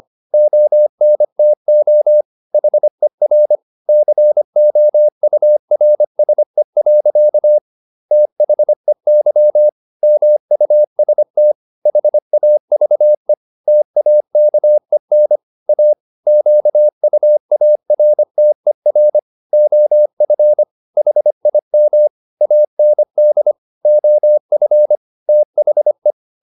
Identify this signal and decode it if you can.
Morse code, heard as a keyed tone, about 25 wpm, 600 Hz. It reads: ONTO HER COURSE. THEY MUST HAVE TAKEN A QUARTER OF HIM AND OF THE